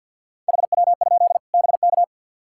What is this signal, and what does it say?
Morse code, heard as a keyed tone, about 50 wpm, 700 Hz. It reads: HW? BK
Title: Practice QSO - Exchange 2 - Info - 50wpm